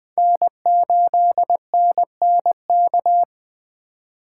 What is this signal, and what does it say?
Morse code, heard as a keyed tone, about 20 wpm, 700 Hz. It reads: N8NNK
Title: USA Call Signs - 5 Characters - 2x Character Spacing - 20wpm